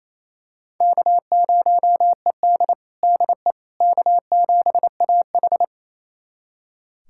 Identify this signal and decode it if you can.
Morse code, heard as a keyed tone, about 28 wpm, 700 Hz. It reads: K0ED DE K7AH
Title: USA Callsign DE Callsign - 4 Characters - 28wpm